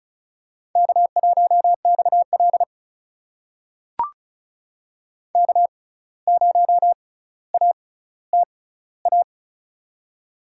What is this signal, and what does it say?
Morse code, heard as a keyed tone, about 36 wpm, spaced as for 17 wpm, 700 Hz. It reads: K1XL K0ATA
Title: USA Call Signs - 6x Character Spacing - 35wpm